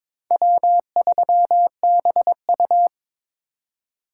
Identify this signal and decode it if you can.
Morse code, heard as a keyed tone, about 22 wpm, 700 Hz. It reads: W3BU
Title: USA Call Signs - 4 Characters - 2x Character Spacing - 22wpm